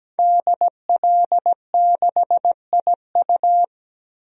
Morse code, heard as a keyed tone, about 17 wpm, 700 Hz. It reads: DL6IU
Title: DXCC-100 Call Signs - 5 Characters - 17wpm